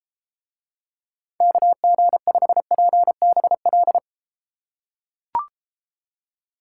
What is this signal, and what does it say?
Morse code, heard as a keyed tone, about 30 wpm, 700 Hz. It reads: KG5PBL